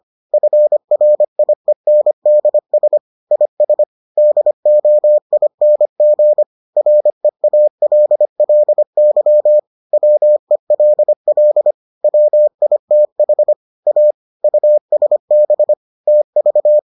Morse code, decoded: FRIENDS IS DOING REALLY WELL WITH A USB TV

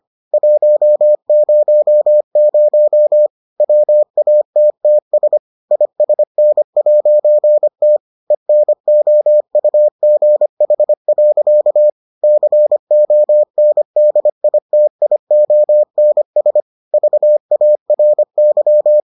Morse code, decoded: 100 WATTS ISN'T ENOUGH. CONDITIONS VARY

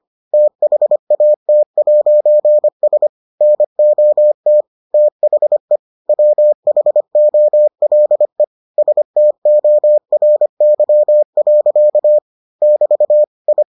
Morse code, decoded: THAT'S NOT THE WHOLE STORY. <BT> I

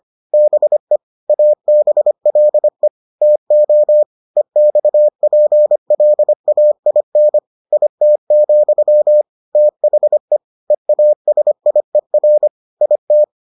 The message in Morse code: BE ABLE TO EXPLAIN IT, THE EASIER IT

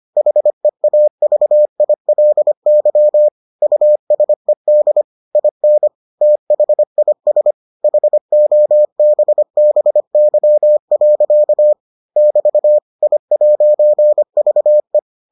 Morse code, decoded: HEAVILY USED IN THIS HOBBY. <BT> I'VE

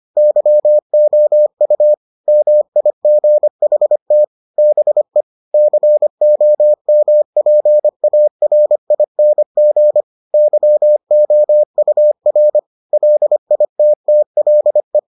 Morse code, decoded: YOU MIGHT BE COMPARING YOUR LITTLE

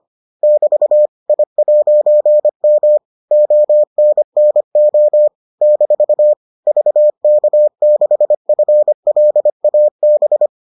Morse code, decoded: <BT> I'M ONNO - VK6FLAB

